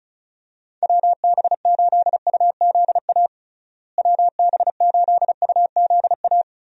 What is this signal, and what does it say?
Morse code, heard as a keyed tone, about 34 wpm, 700 Hz. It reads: WB8UZA WB8UZA
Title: USA Call Signs - 6 Characters - Repeated 2x - 35wpm